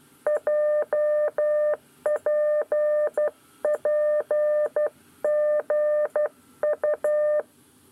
Morse code, heard as a keyed tone, about 12 wpm, 600 Hz. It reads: JPPGU
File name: 9 lezione cw 2